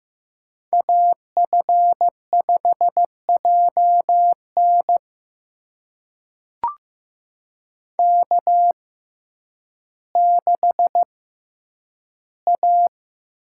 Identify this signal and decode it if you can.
Morse code, heard as a keyed tone, about 15 wpm, spaced as for 9 wpm, 700 Hz. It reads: AF5JN K6A
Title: USA Call Signs - 5 Characters - 6x Character Spacing - 15wpm